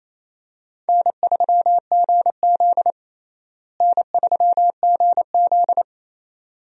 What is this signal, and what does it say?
Morse code, heard as a keyed tone, about 28 wpm, 700 Hz. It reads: N3GZ N3GZ
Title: USA Call Signs - 4 Characters - Repeated 2x - 28wpm